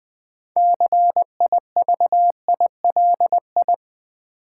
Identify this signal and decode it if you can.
Morse code, heard as a keyed tone, about 20 wpm, 700 Hz. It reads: CIVILI
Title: Build A Word - 6 Letters At A Time - 12-Letter Words - 20wpm